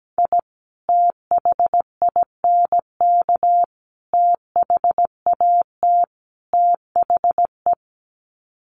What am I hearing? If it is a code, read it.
Morse code, heard as a keyed tone, about 17 wpm, 700 Hz. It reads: I THINK THAT THE
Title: Sets of 4 Words 17wpm